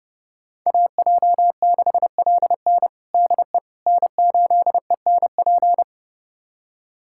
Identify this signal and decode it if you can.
Morse code, heard as a keyed tone, about 32 wpm, 700 Hz. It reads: AJ6LN DE N8ENP
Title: USA Callsign DE Callsign - 5 Characters - 30wpm